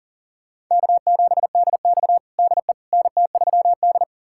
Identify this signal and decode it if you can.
Morse code, heard as a keyed tone, about 40 wpm, 700 Hz. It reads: K7DX DE NT3D
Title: USA Callsign DE Callsign - 4 Characters - 40wpm